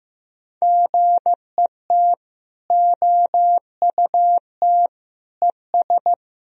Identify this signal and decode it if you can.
Morse code, heard as a keyed tone, about 15 wpm, 700 Hz. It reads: GET OUT ES
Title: Sentences from Top 200 Words and Abbreviations 15wpm